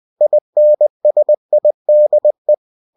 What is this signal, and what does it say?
Morse code, heard as a keyed tone, about 20 wpm, 600 Hz. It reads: INSIDE
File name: cw-books / winnie-the-pooh / winnie-20wpm-page-0049